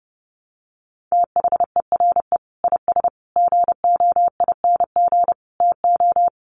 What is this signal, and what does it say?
Morse code, heard as a keyed tone, about 30 wpm, 700 Hz. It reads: THERE IS GOING TO
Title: Sets of 5 Words Encore 30wpm